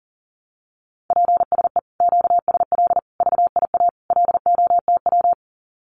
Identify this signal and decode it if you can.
Morse code, heard as a keyed tone, about 40 wpm, 700 Hz. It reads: PSE QSL VIA LOTW
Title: Common QSO Phrases Encore 40wpm